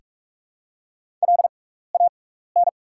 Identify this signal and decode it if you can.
Morse code, heard as a keyed tone, about 44 wpm, spaced as for 15 wpm, 700 Hz. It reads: LAN
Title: 6x Character Spacing - Top 500 Words - 45wpm